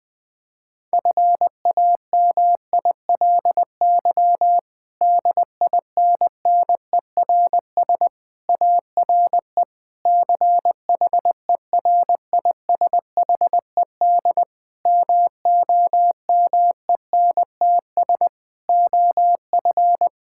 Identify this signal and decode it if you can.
Morse code, heard as a keyed tone, about 20 wpm, 700 Hz. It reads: FAMILY DINNERS ARE CHERISHED MOMENTS OF
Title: Difficult Sentences with Hints - 20wpm